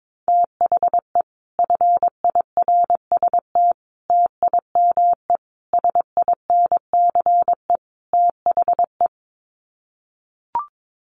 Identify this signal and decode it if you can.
Morse code, heard as a keyed tone, about 22 wpm, 700 Hz. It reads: THE FIRST TIME SINCE THE
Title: Sets of 5 Words Encore 22wpm